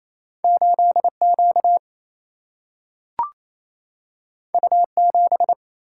Morse code, heard as a keyed tone, about 28 wpm, 700 Hz. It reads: 8 Q U 7
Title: Single-Letter Single-Number 28wpm